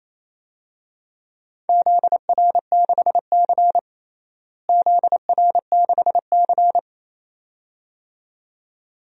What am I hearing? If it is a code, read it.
Morse code, heard as a keyed tone, about 28 wpm, 700 Hz. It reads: ZR6C ZR6C
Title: DXCC-100 Call Signs - 4 Characters - Repeated 2x - 28wpm